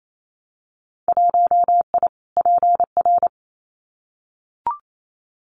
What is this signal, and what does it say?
Morse code, heard as a keyed tone, about 28 wpm, 700 Hz. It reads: 1I PR
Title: Winter Field Day Exchanges Encore 28wpm